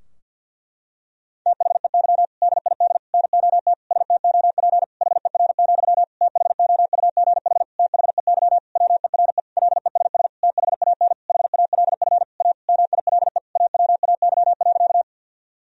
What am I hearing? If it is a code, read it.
Morse code, heard as a keyed tone, about 48 wpm, 700 Hz. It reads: THEY DID NOT STOP HER, THOUGH THEY WERE LESS THAN HALF A MILE AWAY.
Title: Sentences from Top 1000 Words - Review Part 2 - 50wpm